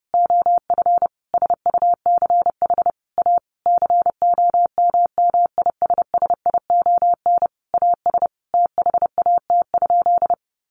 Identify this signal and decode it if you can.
Morse code, heard as a keyed tone, about 28 wpm, 700 Hz. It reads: OF SUCH A COMMISSION AS THAT?
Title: Sentences from Top 1400 Words 30wpm